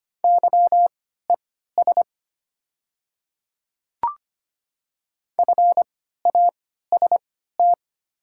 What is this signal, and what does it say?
Morse code, heard as a keyed tone, about 25 wpm, spaced as for 14 wpm, 700 Hz. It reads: YES FAST
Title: 3x Character Spacing - Top 500 Words - 25wpm